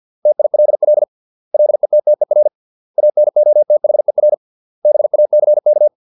Morse code, heard as a keyed tone, about 50 wpm, 600 Hz. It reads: TILL LETTER ANOTHER BACK